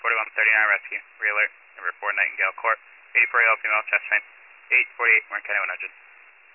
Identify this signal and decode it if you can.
Emergency Alert System (SAME) tones (header bursts), heard as harsh buzzing bursts, with speech. It is heard from 1.0 to 5.5 s.